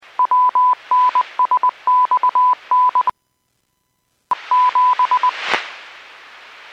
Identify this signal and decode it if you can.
Morse code, heard as a keyed tone, about 20 wpm, 1000 Hz. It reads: WNSXN 7
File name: WPS / 06262 08-53 WPS 451.25 WPS